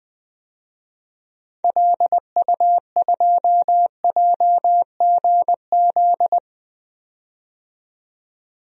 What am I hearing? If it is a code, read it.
Morse code, heard as a keyed tone, about 20 wpm, 700 Hz. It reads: LU2JGZ
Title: DXCC-100 Call Signs - 6 Characters - 20wpm